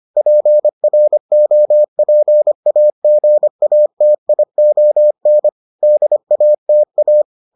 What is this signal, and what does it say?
Morse code, heard as a keyed tone, about 25 wpm, 600 Hz. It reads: PROPAGATION DATA